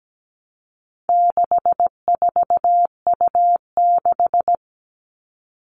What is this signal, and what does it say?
Morse code, heard as a keyed tone, about 17 wpm, 700 Hz. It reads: 64U6
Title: All Cut Numbers - 4-Digit 17wpm